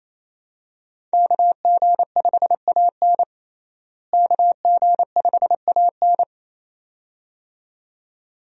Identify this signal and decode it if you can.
Morse code, heard as a keyed tone, about 28 wpm, 700 Hz. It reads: KG5AN KG5AN
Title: USA Call Signs - 5 Characters - Repeated 2x - 28wpm